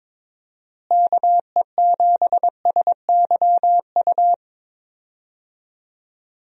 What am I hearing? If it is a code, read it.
Morse code, heard as a keyed tone, about 22 wpm, 700 Hz. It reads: KE7SYU
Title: USA Call Signs - 6 Characters - Repeated 2x - 22wpm